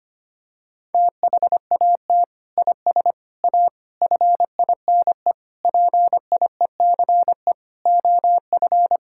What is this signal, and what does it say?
Morse code, heard as a keyed tone, about 25 wpm, 700 Hz. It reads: THAT IS A FINE PIECE OF
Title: Sentences from Top 1000 Words - Review Part 5 - 25wpm